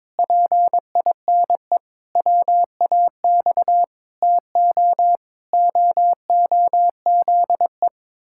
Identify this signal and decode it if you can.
Morse code, heard as a keyed tone, about 22 wpm, 700 Hz. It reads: PINE WAX TO OOZE
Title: Pangram Challenge v2 - 22wpm